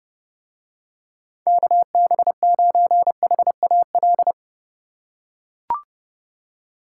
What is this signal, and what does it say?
Morse code, heard as a keyed tone, about 30 wpm, 700 Hz. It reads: KB9HAL